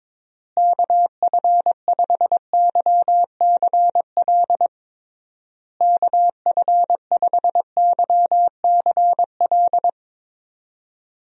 Morse code, heard as a keyed tone, about 22 wpm, 700 Hz. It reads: KF5YCL KF5YCL
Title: USA Call Signs - 6 Characters - Repeated 3x - 22wpm